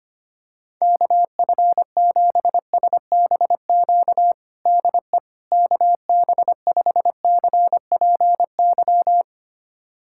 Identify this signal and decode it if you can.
Morse code, heard as a keyed tone, about 25 wpm, 700 Hz. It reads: KF7SBQ DE KB5CPY
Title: USA Callsign DE Callsign 25wpm